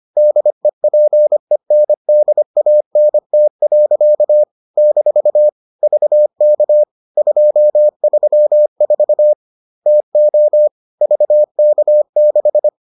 Morse code, heard as a keyed tone, about 25 wpm, 600 Hz. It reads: DEPENDANT. - VK234 TO VK6